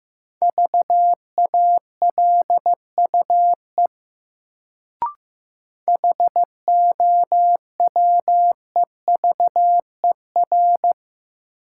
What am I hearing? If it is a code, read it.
Morse code, heard as a keyed tone, about 15 wpm, 700 Hz. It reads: VALUE HOWEVER
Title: Sentences from Top 600 Words - New Word Drill - 15wpm